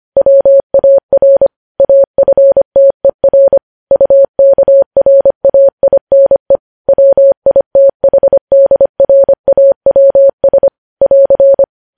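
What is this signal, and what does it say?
Morse code, heard as a keyed tone, about 25 wpm, 550 Hz. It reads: WAR AFTER UKRAINE WITHDRAWS <AR>
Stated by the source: Morse code news headlines at 25 WPM.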